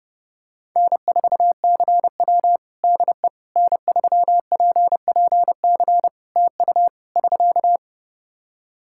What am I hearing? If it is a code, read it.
Morse code, heard as a keyed tone, about 30 wpm, 700 Hz. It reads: N4CW DE N3PPC TU <SK>
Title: Practice QSO - Exchange 4 - Ending - 30wpm